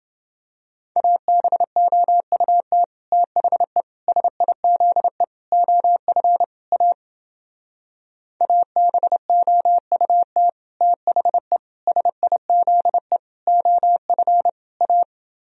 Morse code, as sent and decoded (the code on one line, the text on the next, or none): .- -... --- ..- - / - .... / ... .. --.. / --- ..-. / .- / .- -... --- ..- - / - .... / ... .. --.. / --- ..-. / .-
ABOUT THE SIZE OF A ABOUT THE SIZE OF A